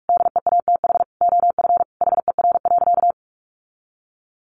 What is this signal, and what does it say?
Morse code, heard as a keyed tone, about 44 wpm, 700 Hz. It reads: DEATH OF HER.
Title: Sentences from Top 1100 Words 45wpm